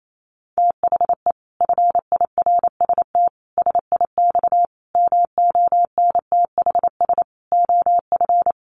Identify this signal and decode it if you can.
Morse code, heard as a keyed tone, about 28 wpm, 700 Hz. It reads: THE FIRST SIX MONTHS OF
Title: Sets of 5 Words Encore2 28wpm